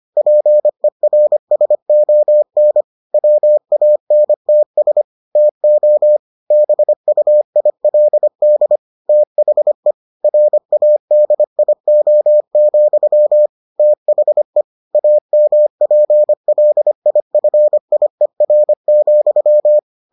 Morse code, heard as a keyed tone, about 25 wpm, 600 Hz. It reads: PERSON WANTS TO BUILD THE RADIO, THE AMPLIFIER,